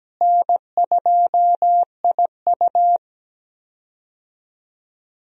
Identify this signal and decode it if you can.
Morse code, heard as a keyed tone, about 17 wpm, 700 Hz. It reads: N2IU
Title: USA Call Signs - 4 Characters - Repeated 2x - 17wpm